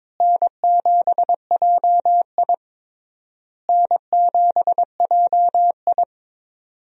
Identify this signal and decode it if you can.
Morse code, heard as a keyed tone, about 22 wpm, 700 Hz. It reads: N7JI N7JI
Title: USA Call Signs - 4 Characters - Repeated 3x - 22wpm